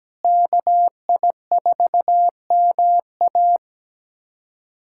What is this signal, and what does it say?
Morse code, heard as a keyed tone, about 17 wpm, 700 Hz. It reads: KI4MA
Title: USA Callsign DE Callsign - 5 Characters - 17wpm